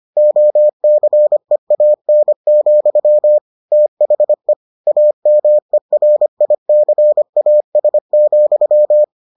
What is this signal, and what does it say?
Morse code, heard as a keyed tone, about 25 wpm, 600 Hz. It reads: OCEAN, THE AMERICAS,